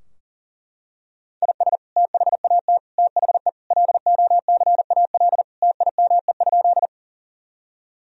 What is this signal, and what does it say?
Morse code, heard as a keyed tone, about 40 wpm, 700 Hz. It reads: IS THAT THE LOCAL TIME?
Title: Sentences from Top 1000 Words - Review Part 3 - 40wpm